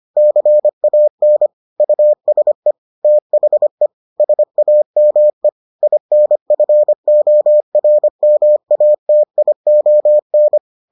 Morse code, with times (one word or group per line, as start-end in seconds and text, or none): 0.2-1.5 CAN
1.8-2.7 USE
3.0-3.9 THE
4.2-5.5 SAME
5.8-10.6 INFORMATION